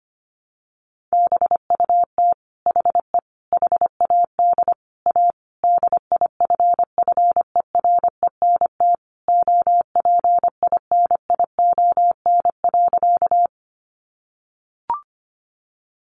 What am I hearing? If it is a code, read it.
Morse code, heard as a keyed tone, about 25 wpm, 700 Hz. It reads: BUT HE HAD A DIFFERENT OPINION.
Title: Sentences from Top 1200 Words 25wpm